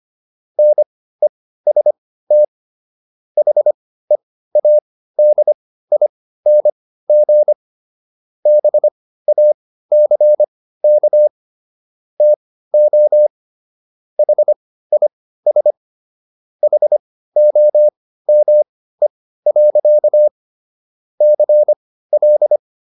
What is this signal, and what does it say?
Morse code, heard as a keyed tone, about 25 wpm, spaced as for 15 wpm, 600 Hz. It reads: NEST HEADING BACK TO HIS HOME. CL